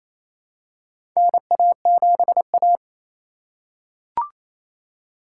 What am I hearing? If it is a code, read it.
Morse code, heard as a keyed tone, about 28 wpm, 700 Hz. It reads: NA7A
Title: USA Call Signs - 4 Characters - 3x Character Spacing - 28wpm